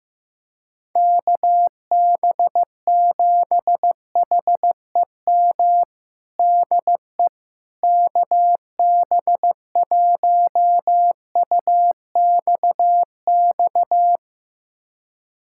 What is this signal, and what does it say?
Morse code, heard as a keyed tone, about 15 wpm, 700 Hz. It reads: KB7HEM DE KB1UXX
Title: USA Callsign DE Callsign - 6 Characters - 15wpm